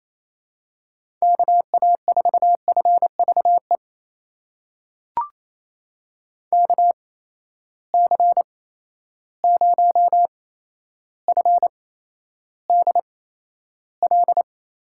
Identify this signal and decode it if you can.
Morse code, heard as a keyed tone, about 28 wpm, spaced as for 12 wpm, 700 Hz. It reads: KA4FVE KC0FDL
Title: USA Call Signs - 6 Characters - 8x Character Spacing - 28wpm